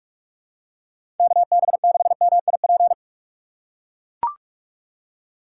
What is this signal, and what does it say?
Morse code, heard as a keyed tone, about 44 wpm, 700 Hz. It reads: KB6MIP
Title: USA Call Signs - 6 Characters - 3x Character Spacing - 45wpm